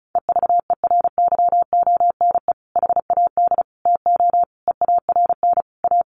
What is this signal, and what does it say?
Morse code, heard as a keyed tone, about 34 wpm, 700 Hz. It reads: EVERYONE HAD TO EARN A
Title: Sentences from Top 1500 Words 35wpm